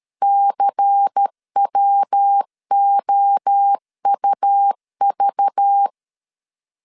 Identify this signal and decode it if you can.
Morse code, heard as a keyed tone, about 13 wpm, 800 Hz. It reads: CWOUV
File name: CW / DET / 1 6 1500C S A 12wpm 3440